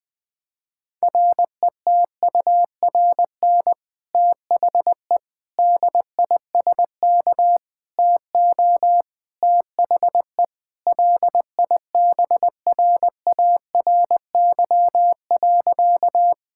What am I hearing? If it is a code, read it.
Morse code, heard as a keyed tone, about 20 wpm, 700 Hz. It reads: RETURN THE DISK TO THE LIBRARY.
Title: Sentences from Top 1000 Words - Review Part 6 - 20wpm